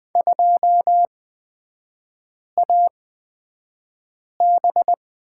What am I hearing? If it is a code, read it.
Morse code, heard as a keyed tone, about 20 wpm, spaced as for 5 wpm, 700 Hz. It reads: 2AB
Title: 250 Lesson 40e - BK - Three Characters 20wpm